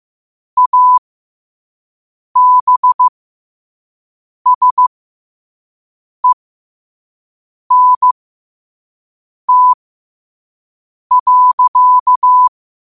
Morse code, heard as a keyed tone, about 15 wpm, spaced as for 5 wpm, 1000 Hz. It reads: ABSENT.
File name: qotdfiles / 5wpm